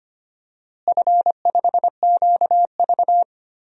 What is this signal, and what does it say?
Morse code, heard as a keyed tone, about 25 wpm, 700 Hz. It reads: F5QV
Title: DXCC-100 Call Signs - 4 Characters - Repeated 3x - 25wpm